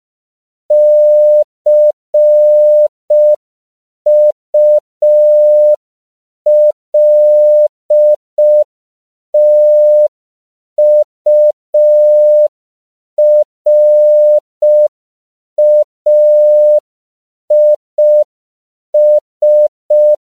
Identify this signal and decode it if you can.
Morse code, heard as a keyed tone, about 5 wpm, 600 Hz. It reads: CULTURAIS
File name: CW / 5PPM